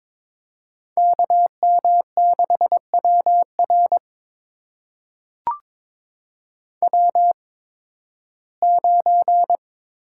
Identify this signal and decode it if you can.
Morse code, heard as a keyed tone, about 22 wpm, spaced as for 14 wpm, 700 Hz. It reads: KM6WR W9
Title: USA Call Signs - 5 Characters - 8x Character Spacing - 22wpm